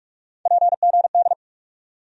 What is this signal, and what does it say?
Morse code, heard as a keyed tone, about 46 wpm, 700 Hz. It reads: PGD
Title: USA Call Sign Postfixes - 3 Letters 45wpm